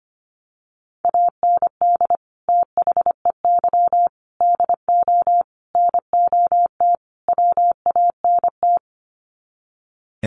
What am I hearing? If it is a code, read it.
Morse code, heard as a keyed tone, about 25 wpm, 700 Hz. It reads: AND THEY DO NOT WANT
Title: Sets of 5 Words Encore2 25wpm